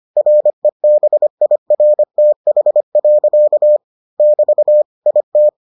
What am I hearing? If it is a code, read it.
Morse code, heard as a keyed tone, about 25 wpm, 600 Hz. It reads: REBIRTH. <BT> IT